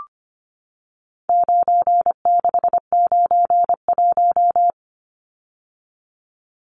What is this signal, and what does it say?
Morse code, heard as a keyed tone, about 25 wpm, 700 Hz. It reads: 9691
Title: Numbers - 4-Digit 25wpm